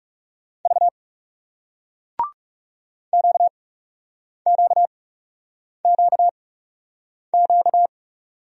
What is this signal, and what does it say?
Morse code, heard as a keyed tone, about 41 wpm, spaced as for 8 wpm, 700 Hz. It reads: V QQQQ